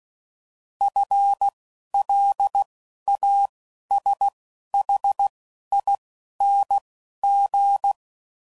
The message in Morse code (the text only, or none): FLASHING